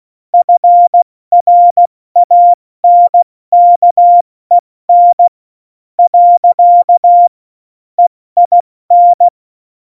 Morse code, heard as a keyed tone, about 16 wpm, spaced as for 14 wpm, 700 Hz. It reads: FRANKEN . EIN
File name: Morsekurs / Witze / Witz 04. Dezember, 70 F80